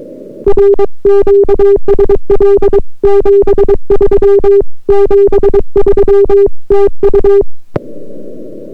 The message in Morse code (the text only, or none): R QSL 73 73 TU